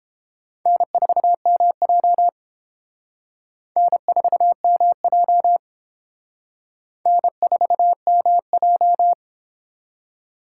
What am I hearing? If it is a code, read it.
Morse code, heard as a keyed tone, about 28 wpm, 700 Hz. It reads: N4MJ N4MJ N4MJ